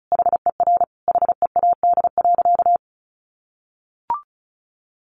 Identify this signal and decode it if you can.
Morse code, heard as a keyed tone, about 34 wpm, 700 Hz. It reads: HER HEAD.
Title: Easy Phrases 35wpm